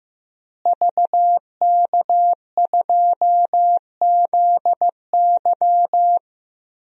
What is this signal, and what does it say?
Morse code, heard as a keyed tone, about 15 wpm, 700 Hz. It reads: VK2ZY
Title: DXCC-100 Call Signs - 5 Characters - Repeated 2x - 15wpm